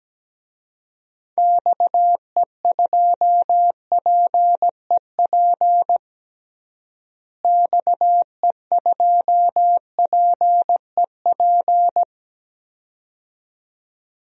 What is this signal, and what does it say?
Morse code, heard as a keyed tone, about 17 wpm, 700 Hz. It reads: XE2PEP XE2PEP
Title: DXCC-100 Call Signs - 6 Characters - Repeated 2x - 17wpm